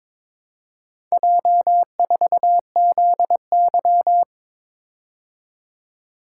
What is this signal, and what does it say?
Morse code, heard as a keyed tone, about 22 wpm, 700 Hz. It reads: J4ZY
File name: practice / 22wpm / Four Letter-Number Combinations - 22wpm